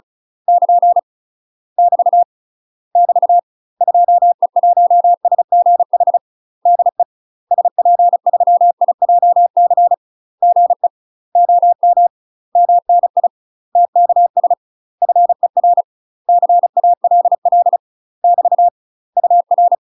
Morse code, heard as a keyed tone, about 35 wpm, 700 Hz. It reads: <KN> <BT> <BT> 2E1SGH DE SP3IJC GE OM MNI TKS FER CALL <BT> UR
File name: UK MASTER QSOs TRAINING/35WPM